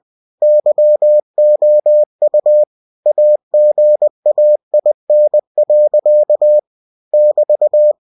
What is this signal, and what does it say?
Morse code, heard as a keyed tone, about 20 wpm, 600 Hz. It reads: YOU AGAIN. <BT>
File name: cw-books / winnie-the-pooh / winnie-20wpm-page-0007